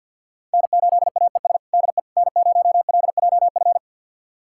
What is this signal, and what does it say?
Morse code, heard as a keyed tone, about 50 wpm, 700 Hz. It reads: N8AES DE N0LJF
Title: USA Callsign DE Callsign - 5 Characters - 50wpm